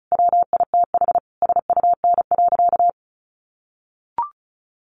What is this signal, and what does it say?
Morse code, heard as a keyed tone, about 34 wpm, 700 Hz. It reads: WITH SUN.
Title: A Sound Of Thunder 35wpm